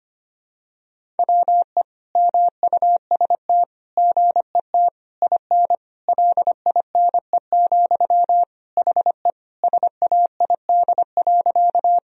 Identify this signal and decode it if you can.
Morse code, heard as a keyed tone, about 25 wpm, 700 Hz. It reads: WE MUST GET IN LINE, HE SAID.
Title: Sentences from Top 1000 Words - Review Part 1 - 25wpm